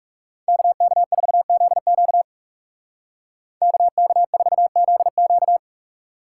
Morse code, heard as a keyed tone, about 36 wpm, 700 Hz. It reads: KK4ZQ KK4ZQ
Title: Speed Racing - USA Call Signs 30wpm